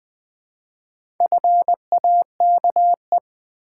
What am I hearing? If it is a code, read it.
Morse code, heard as a keyed tone, about 20 wpm, 700 Hz. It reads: FAKE
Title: Build A Word - 1 Letter At A Time - 4-Letter Words - 20wpm